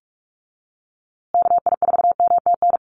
Morse code, heard as a keyed tone, about 44 wpm, 700 Hz. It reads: KI4MTN
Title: USA Call Signs Repeated 2x 45wpm